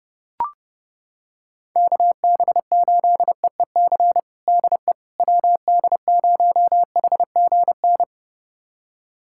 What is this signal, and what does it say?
Morse code, heard as a keyed tone, about 30 wpm, 700 Hz. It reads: KB8EEC DE WD0HGN
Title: USA Callsign DE Callsign - 6 Characters - 30wpm